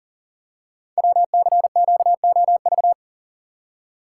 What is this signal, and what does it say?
Morse code, heard as a keyed tone, about 42 wpm, 690 Hz. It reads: WCQOV
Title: Code Groups - 5 Letters - 40wpm